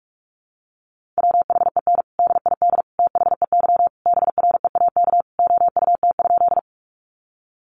Morse code, heard as a keyed tone, about 44 wpm, 700 Hz. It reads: WHEN DID THEY BREAK OUT?
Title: Sentences from Top 900 Words 45wpm